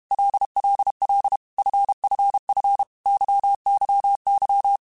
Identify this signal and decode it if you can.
Morse code, heard as a keyed tone, about 32 wpm, 800 Hz. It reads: LLL FFF YYY
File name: Groups@30 / Groups@30 mp3s / LFCYQ@30